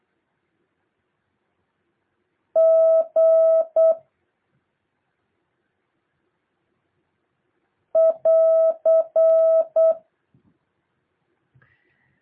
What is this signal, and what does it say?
Morse code, heard as a keyed tone, about 8 wpm, 650 Hz. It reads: G <AR>
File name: jh8cbh / 2023cw